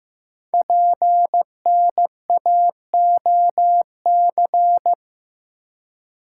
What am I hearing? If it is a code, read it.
Morse code, heard as a keyed tone, about 15 wpm, 700 Hz. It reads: PNAOC
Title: Code Groups - 5 Letters - 15wpm